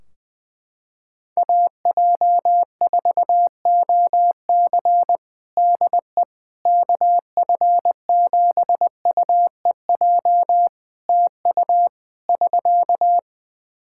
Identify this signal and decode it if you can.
Morse code, heard as a keyed tone, about 20 wpm, 700 Hz. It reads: AJ4OC DE KF7UEJ TU <SK>
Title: Practice QSO - Exchange 4 - Ending - 20wpm